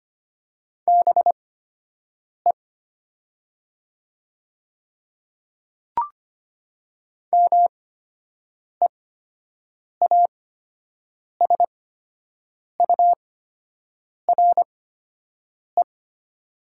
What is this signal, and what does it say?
Morse code, heard as a keyed tone, about 25 wpm, spaced as for 7 wpm, 700 Hz. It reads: BE MEASURE
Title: 8x Character Spacing - Top 500 Words - 25wpm